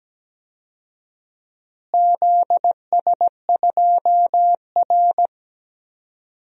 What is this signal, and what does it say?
Morse code, heard as a keyed tone, about 17 wpm, 700 Hz. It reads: ZS2R
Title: DXCC-100 Call Signs - 4 Characters - Repeated 3x - 17wpm